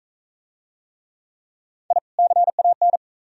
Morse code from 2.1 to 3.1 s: CAN

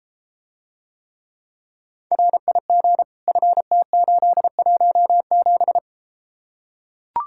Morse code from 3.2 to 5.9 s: FT817

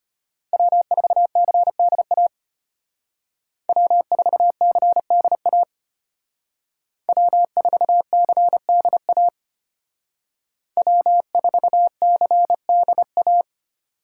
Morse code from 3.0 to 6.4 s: W4CDA

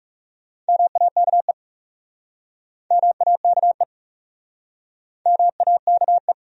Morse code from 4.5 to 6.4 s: MAKE

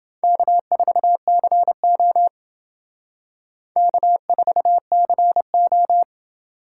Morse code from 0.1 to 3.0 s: K4CO

